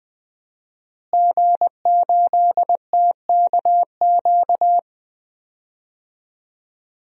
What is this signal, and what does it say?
Morse code, heard as a keyed tone, about 20 wpm, 700 Hz. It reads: G8TKQ
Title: DXCC-100 Call Signs - 5 Characters - Repeated 2x - 20wpm